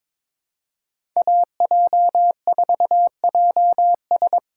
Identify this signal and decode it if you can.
Morse code, heard as a keyed tone, about 22 wpm, 700 Hz. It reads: AJ4JS
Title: USA Call Signs - 5 Characters - 4x Character Spacing - 22wpm